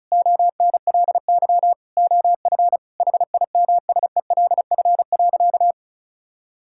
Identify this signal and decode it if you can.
Morse code, heard as a keyed tone, about 34 wpm, 700 Hz. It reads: ONLY OF HIMSELF.
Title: Sentences from Top 1600 Words 35wpm